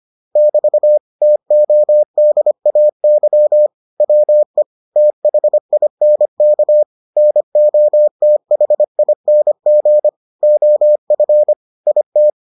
Morse code, heard as a keyed tone, about 25 wpm, 600 Hz. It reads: <BT> TODAY WE THINK NOTHING OF IT